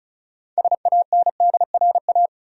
Morse code, heard as a keyed tone, about 36 wpm, 700 Hz. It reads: SANDRA
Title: USA Common Names - 35wpm